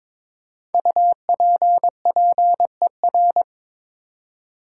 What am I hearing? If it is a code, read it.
Morse code, heard as a keyed tone, about 23 wpm, 700 Hz. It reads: UPPER
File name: practice / 22wpm / 5 Letter Words - First 3 Letters Given - 22wpm